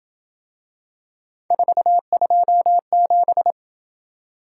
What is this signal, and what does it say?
Morse code, heard as a keyed tone, about 26 wpm, 700 Hz. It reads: 427